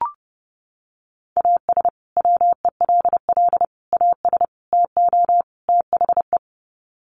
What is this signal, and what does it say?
Morse code, heard as a keyed tone, about 30 wpm, 700 Hz. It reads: AS WELL AS TO THE
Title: Sets of 5 Words Encore2 30wpm